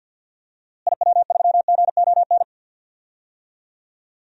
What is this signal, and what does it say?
Morse code, heard as a keyed tone, about 52 wpm, 700 Hz. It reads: IW3GON
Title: DXCC-100 Call Signs - 6 Characters - Repeated 3x - 50wpm